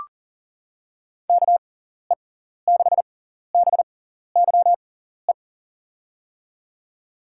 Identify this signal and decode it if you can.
Morse code, heard as a keyed tone, about 39 wpm, spaced as for 14 wpm, 700 Hz. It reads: KE6BYE
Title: USA Call Signs - 6 Characters - 6x Character Spacing - 40wpm